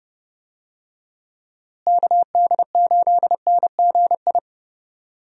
Morse code, heard as a keyed tone, about 30 wpm, 700 Hz. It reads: KD8NGI
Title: USA Call Signs - 6 Characters - Repeated 2x - 30wpm